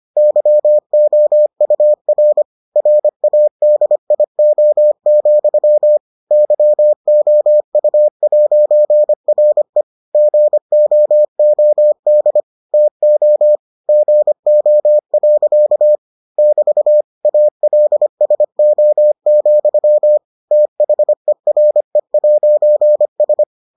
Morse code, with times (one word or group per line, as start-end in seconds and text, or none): 0.2-2.4 YOUR
2.8-6.0 RADIO,
6.3-9.8 YOU'RE
10.1-12.4 GOOD
12.7-13.6 TO
13.9-16.0 GO.
16.4-17.0 <BT>
17.2-20.2 ALSO,
20.5-23.4 THERE'S